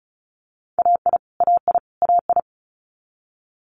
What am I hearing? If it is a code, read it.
Morse code, heard as a keyed tone, about 36 wpm, 700 Hz. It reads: AI AI AI
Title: Multi-letter Phonemes Repeated 3x 35wpm